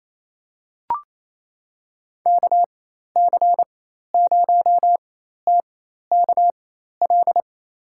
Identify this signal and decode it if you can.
Morse code, heard as a keyed tone, about 28 wpm, spaced as for 13 wpm, 700 Hz. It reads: KC0TKL
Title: USA Call Signs - 6 Characters - 4x Character Spacing - 28wpm